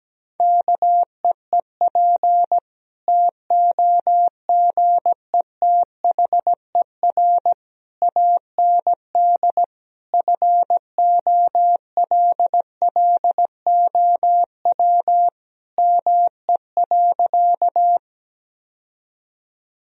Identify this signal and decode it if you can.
Morse code, heard as a keyed tone, about 17 wpm, 700 Hz. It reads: KEEP TOGETHER AND FOLLOW ME.
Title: Sentences from Top 1000 Words - Review Part 1 - 17wpm